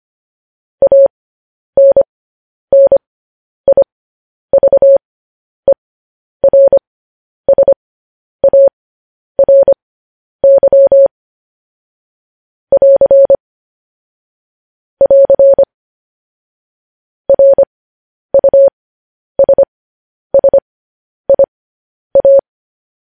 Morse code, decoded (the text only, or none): ANNIVERSARY <AR> <AR> RUSSIA